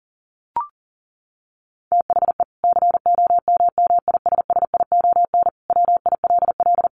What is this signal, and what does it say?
Morse code, heard as a keyed tone, about 40 wpm, 700 Hz. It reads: THE COMMISSION WILL
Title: Sentences from Top 1400 Words 40wpm